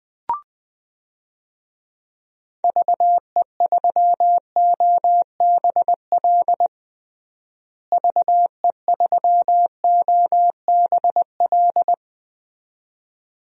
Morse code, heard as a keyed tone, about 20 wpm, 700 Hz. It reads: VE3OBL VE3OBL
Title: DXCC-100 Call Signs - 6 Characters - Repeated 2x - 20wpm